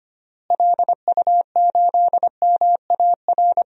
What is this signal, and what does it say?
Morse code, heard as a keyed tone, about 25 wpm, 700 Hz. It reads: LU8MAR
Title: DXCC-100 Call Signs - 6 Characters - Repeated 2x - 25wpm